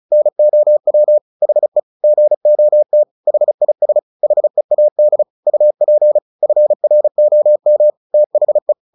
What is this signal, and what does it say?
Morse code, heard as a keyed tone, about 36 wpm, 600 Hz. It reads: NOW HE GOT HIS HEAD UP FROM THE
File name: the old man and the sea 2 35wpm